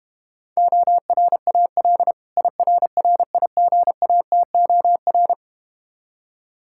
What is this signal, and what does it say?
Morse code, heard as a keyed tone, about 32 wpm, 700 Hz. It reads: ORAL IRRIGATOR